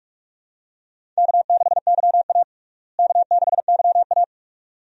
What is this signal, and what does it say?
Morse code, heard as a keyed tone, about 47 wpm, 700 Hz. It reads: K6YA K6YA
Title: USA Call Signs - 4 Characters - Repeated 2x - 45wpm